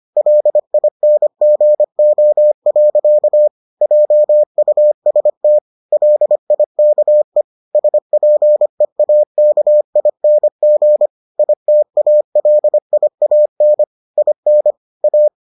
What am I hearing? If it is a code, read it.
Morse code, heard as a keyed tone, about 25 wpm, 600 Hz. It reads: LINGO. JUST LIKE SPEAKING ITALIAN IN A